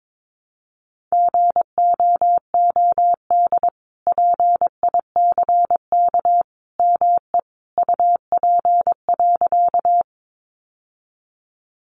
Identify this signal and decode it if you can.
Morse code, heard as a keyed tone, about 22 wpm, 700 Hz. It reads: GOOD PICK ME UP.
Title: Sentences from Top 1000 Words 22wpm